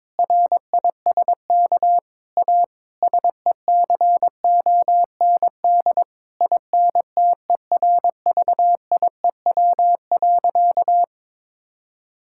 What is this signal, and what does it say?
Morse code, heard as a keyed tone, about 22 wpm, 700 Hz. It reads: RISK A SECOND INTERVIEW.
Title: Sentences from Top 1600 Words 22wpm